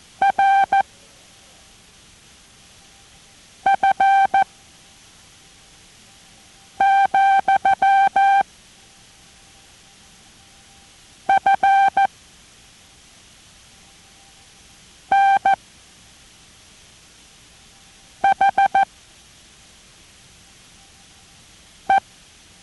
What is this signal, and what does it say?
Morse code, heard as a keyed tone, about 14 wpm, spaced as for 3 wpm, 780 Hz. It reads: RF, FNHE